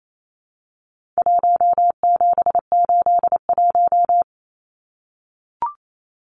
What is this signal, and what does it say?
Morse code, heard as a keyed tone, about 28 wpm, 700 Hz. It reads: 1781
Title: Numbers - Contest Style - 4-Digit 28wpm